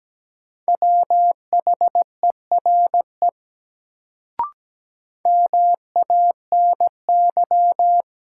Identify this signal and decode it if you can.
Morse code, heard as a keyed tone, about 17 wpm, 700 Hz. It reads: WHERE MANY
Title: Sets of 2 Words - New Word Drill - 17wpm